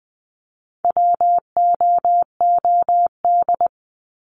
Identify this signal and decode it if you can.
Morse code, heard as a keyed tone, about 20 wpm, 700 Hz. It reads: WOOD
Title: Top 300 Words 20wpm